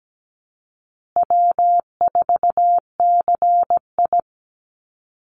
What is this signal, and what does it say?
Morse code, heard as a keyed tone, about 17 wpm, 700 Hz. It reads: W4CI
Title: Call Signs Repeated 2x - 17wpm